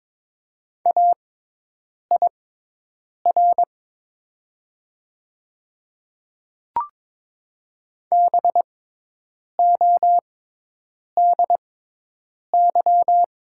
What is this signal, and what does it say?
Morse code, heard as a keyed tone, about 22 wpm, spaced as for 8 wpm, 700 Hz. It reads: AIR BODY
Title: 6x Character Spacing - Top 500 Words - 22wpm